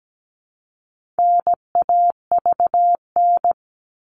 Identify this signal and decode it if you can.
Morse code, heard as a keyed tone, about 17 wpm, 700 Hz. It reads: NAVN
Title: All Cut Numbers - 4-Digit 17wpm